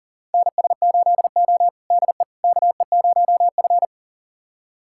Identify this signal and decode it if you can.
Morse code, heard as a keyed tone, about 40 wpm, 700 Hz. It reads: NS8O DE KE0F
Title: USA Callsign DE Callsign - 4 Characters - 40wpm